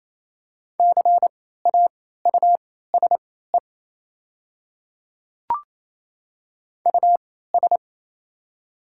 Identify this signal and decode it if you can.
Morse code, heard as a keyed tone, about 28 wpm, spaced as for 16 wpm, 700 Hz. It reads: CAUSE US
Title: 3x Character Spacing - Top 200 Words - 28wpm